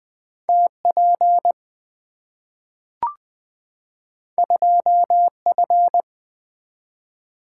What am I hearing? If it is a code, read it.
Morse code, heard as a keyed tone, about 20 wpm, 700 Hz. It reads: TP 2F
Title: Two Letter-Number Combinations - 20wpm